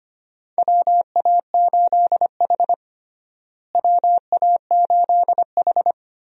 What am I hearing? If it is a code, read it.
Morse code, heard as a keyed tone, about 25 wpm, 700 Hz. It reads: WA8H WA8H
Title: USA Call Signs - 4 Characters - Repeated 3x - 25wpm